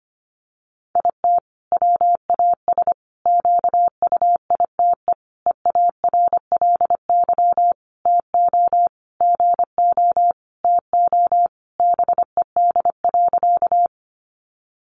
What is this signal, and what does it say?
Morse code, heard as a keyed tone, about 25 wpm, 700 Hz. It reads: IT WAS QUITE EARLY TO GO TO BED.
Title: Sentences from Top 700 Words 25wpm